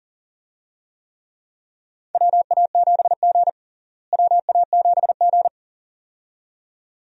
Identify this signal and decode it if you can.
Morse code, heard as a keyed tone, about 37 wpm, 700 Hz. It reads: WA7G WA7G
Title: USA Call Signs - 4 Characters - Repeated 2x - 40wpm